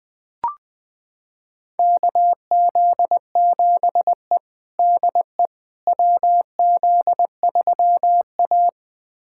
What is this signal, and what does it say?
Morse code, heard as a keyed tone, about 21 wpm, 700 Hz. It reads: KZ7E DE WZ3A
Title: USA Callsign DE Callsign - 4 Characters - 20wpm